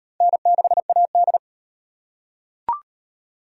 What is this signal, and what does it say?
Morse code, heard as a keyed tone, about 35 wpm, 700 Hz. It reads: N6AD